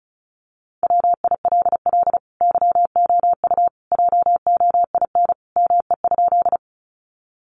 Morse code, heard as a keyed tone, about 34 wpm, 700 Hz. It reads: WILL YOU JOIN ME?
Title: Sentences from Top 1400 Words 35wpm